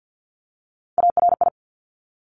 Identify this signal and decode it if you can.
Morse code, heard as a keyed tone, about 50 wpm, 700 Hz. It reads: ARI
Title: Top Three Letter Combinations in Words 50wpm